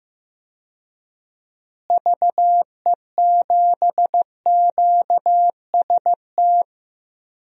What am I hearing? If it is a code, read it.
Morse code, heard as a keyed tone, about 15 wpm, 700 Hz. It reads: VE7QST
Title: DXCC-100 Call Signs - 6 Characters - Repeated 3x - 15wpm